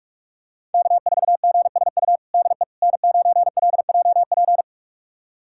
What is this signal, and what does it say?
Morse code, heard as a keyed tone, about 42 wpm, 700 Hz. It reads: K4GSU DE N9LJP
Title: USA Callsign DE Callsign 45wpm